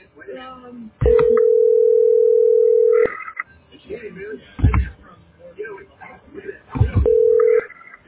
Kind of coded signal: telephone call progress tone